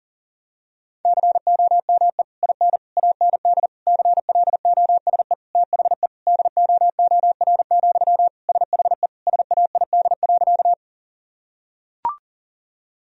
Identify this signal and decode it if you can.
Morse code, heard as a keyed tone, about 40 wpm, 700 Hz. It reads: COME IN AND CLOSE THE DOOR, SHE SAID.
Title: Sentences from Top 1000 Words - Review Part 1 - 40wpm